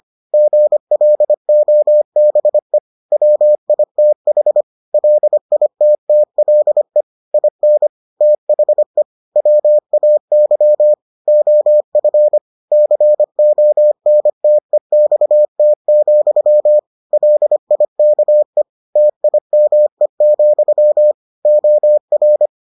Morse code, heard as a keyed tone, about 25 wpm, 600 Hz. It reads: GLOBE WITH LITTLE IN THE WAY OF CONTEXT, LIKE TIME, OR